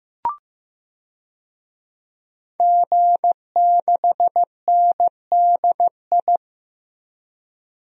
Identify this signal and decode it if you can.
Morse code, heard as a keyed tone, about 15 wpm, 700 Hz. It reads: G6NDI
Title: DXCC-100 Call Signs - 5 Characters - 15wpm